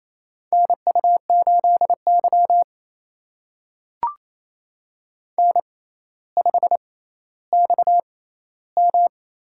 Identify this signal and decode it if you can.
Morse code, heard as a keyed tone, about 27 wpm, spaced as for 14 wpm, 700 Hz. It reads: NU8Y N5XM
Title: USA Call Signs - 4 Characters - 6x Character Spacing - 28wpm